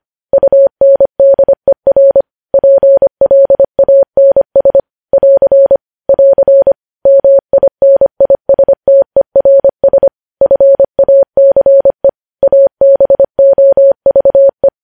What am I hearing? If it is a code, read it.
Morse code, heard as a keyed tone, about 25 wpm, 550 Hz. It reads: UNDER PLANS <AR> <AR> MINISTERS FACE ABOVE
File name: podcast / News headlines-1x-25x25